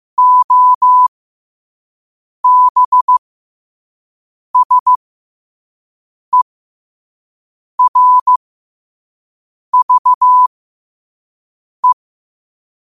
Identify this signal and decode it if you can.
Morse code, heard as a keyed tone, about 15 wpm, spaced as for 5 wpm, 1000 Hz. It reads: OBSERVE
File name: qotdfiles / 5wpm